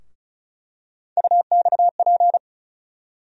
Morse code, heard as a keyed tone, about 34 wpm, 700 Hz. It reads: UXP
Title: USA Call Sign Postfixes - 3 Letters 35wpm